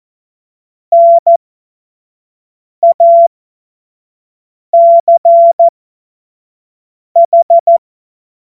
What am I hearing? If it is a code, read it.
Morse code, heard as a keyed tone, about 14 wpm, spaced as for 5 wpm, 680 Hz. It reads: NACH
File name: Morsekurs / Witze / Witz 20. Juni 2023 25 F70